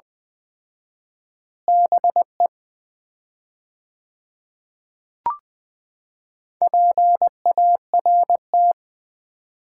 Morse code, heard as a keyed tone, about 21 wpm, 700 Hz. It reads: BE PART
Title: 4x Word-Spacing - Sets of 3 Words - 20wpm